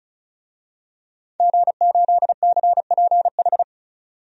Morse code, heard as a keyed tone, about 34 wpm, 700 Hz. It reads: G8CPH
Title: DXCC-100 Call Signs - 5 Characters - 35wpm